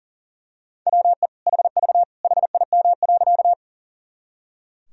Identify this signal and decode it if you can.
Morse code, heard as a keyed tone, about 40 wpm, 700 Hz. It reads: WE HV HIM.
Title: Sentences from Top 100 Words and Abbreviations 40wpm